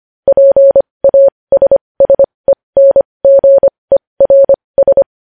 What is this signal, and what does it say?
Morse code, heard as a keyed tone, about 25 wpm, spaced as for 20 wpm, 550 Hz. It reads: PASSENGERS
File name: podcast / News headlines-1x-25x20